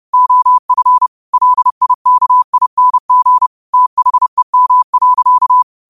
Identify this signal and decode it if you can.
Morse code, heard as a keyed tone, about 30 wpm, 1000 Hz. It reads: OF LIKING THEM.
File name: qotdfiles / 30wpm